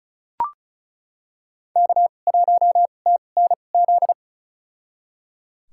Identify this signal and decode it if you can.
Morse code, heard as a keyed tone, about 36 wpm, spaced as for 25 wpm, 700 Hz. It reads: K1TNZ
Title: USA Call Signs - 5 Characters - 2x Character Spacing - 35wpm